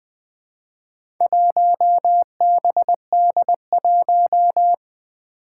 Morse code, heard as a keyed tone, about 20 wpm, 700 Hz. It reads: 1BD1
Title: Four Challenging Letter-Number Combinations 20wpm